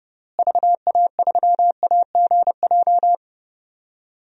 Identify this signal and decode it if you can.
Morse code, heard as a keyed tone, about 30 wpm, 700 Hz. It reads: VA3AGJ
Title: DXCC-100 Call Signs - 6 Characters - Repeated 3x - 30wpm